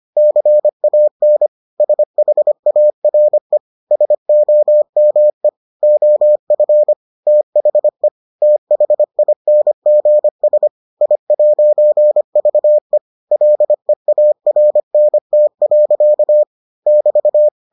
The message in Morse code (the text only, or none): CAN SHARE SOME OF THE THINGS I'VE LEARNT. <BT>